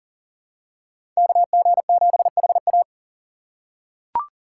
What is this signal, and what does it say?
Morse code, heard as a keyed tone, about 40 wpm, 700 Hz. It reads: KG7HA
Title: USA Call Signs - 3x Character Spacing - 40wpm